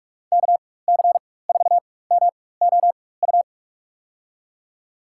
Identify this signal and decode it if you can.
Morse code, heard as a keyed tone, about 45 wpm, spaced as for 21 wpm, 700 Hz. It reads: KC4MOU
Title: USA Call Signs - 6 Characters - 4x Character Spacing - 45wpm